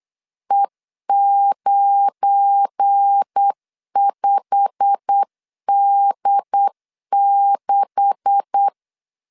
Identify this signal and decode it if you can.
Morse code, heard as a keyed tone, about 8 wpm, 800 Hz. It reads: E95D6
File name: LEZIONI / 9 286 8wpm 3425